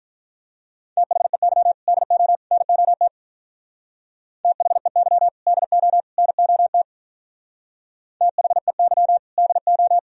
Morse code, heard as a keyed tone, about 44 wpm, 700 Hz. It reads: THEY DO NOT THEY DO NOT THEY DO